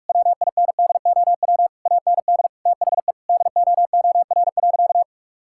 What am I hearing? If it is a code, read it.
Morse code, heard as a keyed tone, about 46 wpm, 700 Hz. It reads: WINDOW AND THE DOOR.
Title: Sentences from Top 1600 Words 45wpm